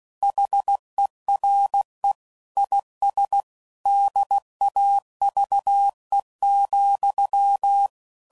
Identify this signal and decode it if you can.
Morse code, heard as a keyed tone, about 16 wpm, 800 Hz. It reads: HERE IS DAVE,
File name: CW Code Groups @15WPM / QSOs / WD0BGP Spaced